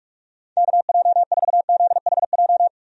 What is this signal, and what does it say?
Morse code, heard as a keyed tone, about 46 wpm, 700 Hz. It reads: KJ4ZHJ
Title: USA Call Signs - 6 Characters - 45wpm